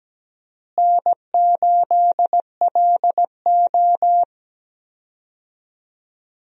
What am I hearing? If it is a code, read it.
Morse code, heard as a keyed tone, about 17 wpm, 700 Hz. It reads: N8LO
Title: USA Call Signs - 4 Characters - Repeated 2x - 17wpm